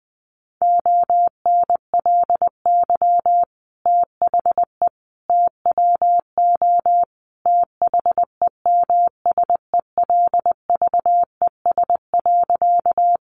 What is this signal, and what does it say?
Morse code, heard as a keyed tone, about 20 wpm, 700 Hz. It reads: ONLY THE TWO THEMSELVES.
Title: Sentences from Top 800 Words 20wpm